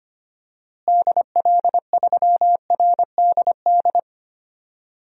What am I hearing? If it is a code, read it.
Morse code, heard as a keyed tone, about 25 wpm, 700 Hz. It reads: DL3RDD
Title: DXCC-100 Call Signs - 6 Characters - Repeated 3x - 25wpm